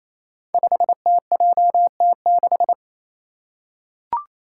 Morse code, heard as a keyed tone, about 27 wpm, 700 Hz. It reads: 5TJT6
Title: Five Challenging Letter-Number Combinations 28wpm